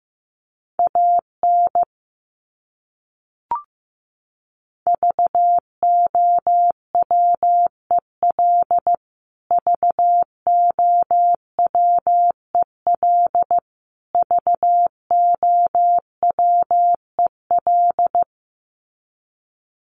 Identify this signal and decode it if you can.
Morse code, heard as a keyed tone, about 15 wpm, 700 Hz. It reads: AN VOWEL VOWEL VOWEL
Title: Top 500 Words Repeated 15wpm